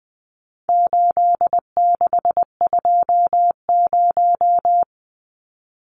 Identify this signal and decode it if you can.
Morse code, heard as a keyed tone, about 20 wpm, 700 Hz. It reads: 8620
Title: Numbers - 4-Digit 20wpm